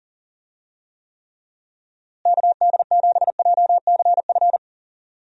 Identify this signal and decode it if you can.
Morse code, heard as a keyed tone, about 37 wpm, 700 Hz. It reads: KD7JCF
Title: USA Call Signs - 6 Characters - Repeated 2x - 40wpm